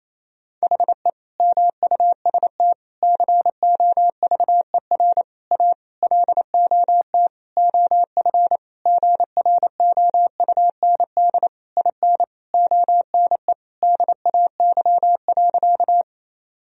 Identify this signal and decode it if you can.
Morse code, heard as a keyed tone, about 27 wpm, 700 Hz. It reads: HE MUST COVER A LOT OF GROUND IN ONE DAY.
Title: Sentences from Top 1000 Words - Review Part 3 - 28wpm